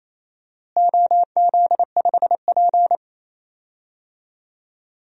Morse code, heard as a keyed tone, about 28 wpm, 700 Hz. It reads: OZ5P
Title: DXCC-100 Call Signs - 4 Characters - Repeated 3x - 28wpm